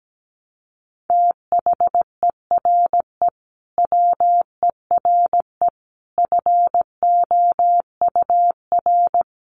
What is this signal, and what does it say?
Morse code, heard as a keyed tone, about 17 wpm, 700 Hz. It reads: THERE WERE FOUR
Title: A Sound Of Thunder 17wpm